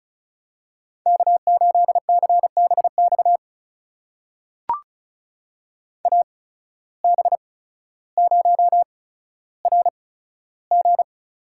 Morse code, heard as a keyed tone, about 36 wpm, spaced as for 15 wpm, 700 Hz. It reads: K8CBX AB0RG
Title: USA Call Signs - 5 Characters - 8x Character Spacing - 35wpm